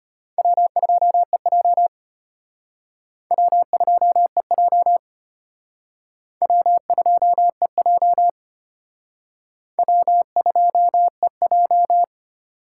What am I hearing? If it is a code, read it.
Morse code, heard as a keyed tone, about 31 wpm, 700 Hz. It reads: W2EJ W2EJ W2EJ W2EJ
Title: Speed Racing - USA 1x2 Call Signs 25wpm